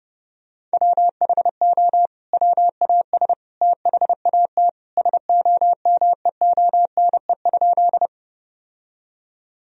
Morse code, heard as a keyed tone, about 32 wpm, 700 Hz. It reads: WHO WAS THAT SOMEONE?
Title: Sentences from Top 1000 Words - Review Part 4 - 30wpm